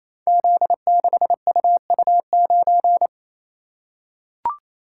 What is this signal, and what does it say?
Morse code, heard as a keyed tone, about 27 wpm, 700 Hz. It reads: Z6UU9
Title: Code Groups - 5 Letter:Numbers - 28wpm